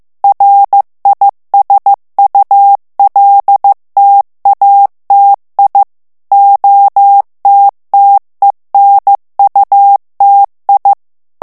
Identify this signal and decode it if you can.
Morse code, heard as a keyed tone, about 15 wpm, 800 Hz. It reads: RISULTATI OTTENUTI
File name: morse mp3 / ITALIEN texte clair 2 vitesse 14 mots minute